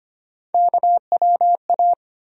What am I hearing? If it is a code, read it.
Morse code, heard as a keyed tone, about 25 wpm, 700 Hz. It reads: KWA
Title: USA Call Sign Postfixes - 3 Letters 25wpm